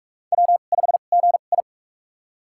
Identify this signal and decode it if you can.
Morse code, heard as a keyed tone, about 43 wpm, spaced as for 32 wpm, 700 Hz. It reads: W5GI
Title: USA Call Signs - 4 Characters - 2x Character Spacing - 45wpm